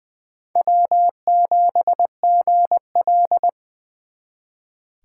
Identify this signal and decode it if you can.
Morse code, heard as a keyed tone, about 23 wpm, 700 Hz. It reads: W7GL
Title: Speed Racing - USA 1x2 Call Signs 20wpm